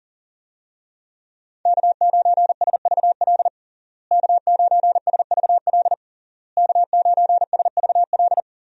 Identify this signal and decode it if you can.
Morse code, heard as a keyed tone, about 40 wpm, 700 Hz. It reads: K9SVL K9SVL K9SVL
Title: USA Call Signs - 5 Characters - Repeated 3x - 40wpm